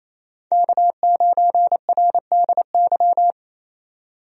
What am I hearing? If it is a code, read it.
Morse code, heard as a keyed tone, about 27 wpm, 700 Hz. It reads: K9RDY
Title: USA Call Signs - 5 Characters - 28wpm